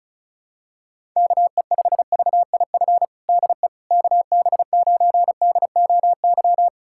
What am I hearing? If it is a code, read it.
Morse code, heard as a keyed tone, about 36 wpm, 700 Hz. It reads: KE5VIF DE KB9DOY
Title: USA Callsign DE Callsign - 6 Characters - 35wpm